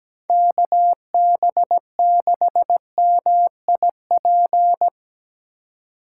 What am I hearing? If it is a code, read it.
Morse code, heard as a keyed tone, about 17 wpm, 700 Hz. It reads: KB6MIP
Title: USA Call Signs - 4x Character Spacing - 17wpm